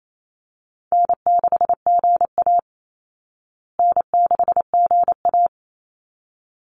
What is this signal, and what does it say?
Morse code, heard as a keyed tone, about 28 wpm, 700 Hz. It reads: N6GA N6GA
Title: Easy Call Signs Repeated 2x - 28wpm